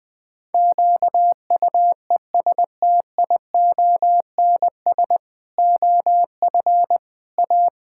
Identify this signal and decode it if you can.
Morse code, heard as a keyed tone, about 20 wpm, 700 Hz. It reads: QUESTIONS OF A
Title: Pangram Challenge v2 - 20wpm